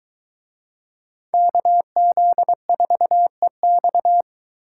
Morse code, heard as a keyed tone, about 23 wpm, 700 Hz. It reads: KZ4EX